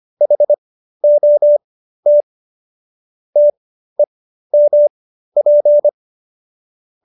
Morse code, heard as a keyed tone, about 25 wpm, spaced as for 13 wpm, 590 Hz. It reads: HOT TEMP